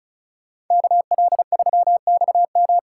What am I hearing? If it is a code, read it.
Morse code, heard as a keyed tone, about 43 wpm, 700 Hz. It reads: KL3XM
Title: Speed Racing - USA Call Signs 35wpm